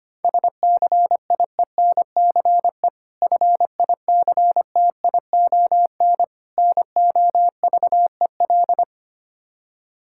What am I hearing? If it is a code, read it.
Morse code, heard as a keyed tone, about 25 wpm, 700 Hz. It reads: SCIENCE FICTION NOVEL
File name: practice / 25wpm / Difficult Words with Hints - 25wpm